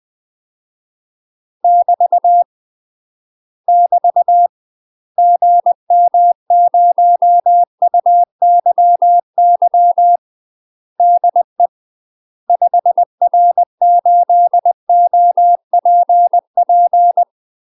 Morse code, heard as a keyed tone, about 20 wpm, 700 Hz. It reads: <BT> <BT> GM0UYY DE 5R8OPP
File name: UK MASTER QSOs TRAINING / 20WPM / UK MASTER QSOs 0072